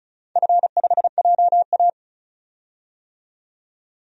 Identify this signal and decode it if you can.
Morse code, heard as a keyed tone, about 36 wpm, 700 Hz. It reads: F5JA
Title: DXCC-100 Call Signs - 4 Characters - 35wpm